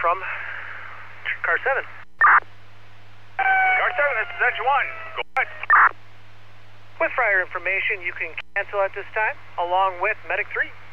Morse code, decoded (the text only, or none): none